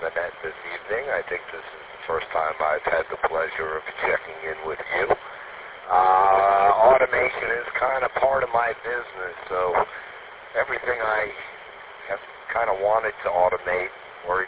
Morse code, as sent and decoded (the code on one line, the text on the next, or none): none